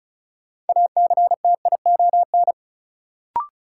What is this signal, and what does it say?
Morse code, heard as a keyed tone, about 34 wpm, 700 Hz. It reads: ACTION
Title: Sentences from Top 700 Words - New Word Drill - 35wpm